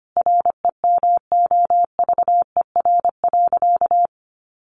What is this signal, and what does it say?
Morse code, heard as a keyed tone, about 25 wpm, 700 Hz. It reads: REMOVER.
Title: Additional Easy Phrases 25wpm